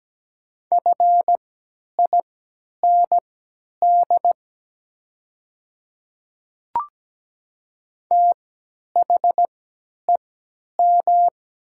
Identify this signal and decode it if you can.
Morse code, heard as a keyed tone, about 17 wpm, spaced as for 10 wpm, 700 Hz. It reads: FIND THEM
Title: 3x Character Spacing - Top 100 Words - 17wpm